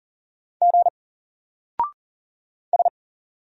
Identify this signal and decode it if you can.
Morse code, heard as a keyed tone, about 40 wpm, 700 Hz. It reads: G S
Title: Single Character (Letter:Number:Punctuation:Prosign) 40wpm